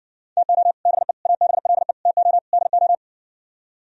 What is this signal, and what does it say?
Morse code, heard as a keyed tone, about 60 wpm, 700 Hz. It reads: TO BE ABLE TO DO